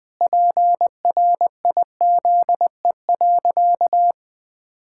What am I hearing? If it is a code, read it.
Morse code, heard as a keyed tone, about 20 wpm, 700 Hz. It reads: PRIZE.
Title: Pangram Challenge v2 - 20wpm